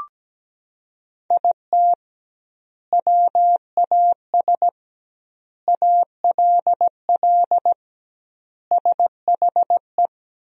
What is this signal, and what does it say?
Morse code, heard as a keyed tone, about 17 wpm, 700 Hz. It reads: IT WAS ALL SHE
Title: 2x Word-Spacing - Sentences from Top 400 Words - 17wpm